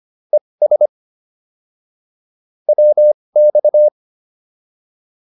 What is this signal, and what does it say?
Morse code, heard as a keyed tone, about 25 wpm, spaced as for 20 wpm, 600 Hz. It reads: ES WX